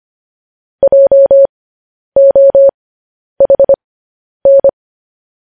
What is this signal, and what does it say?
Morse code, heard as a keyed tone, about 25 wpm, spaced as for 10 wpm, 550 Hz. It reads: JOHN
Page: news headlines in morse code at 10 wpm